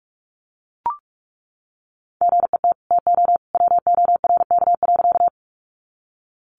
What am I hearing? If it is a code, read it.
Morse code, heard as a keyed tone, about 44 wpm, 700 Hz. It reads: GET TO WORK.
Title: Sentences from Top 200 Words 45wpm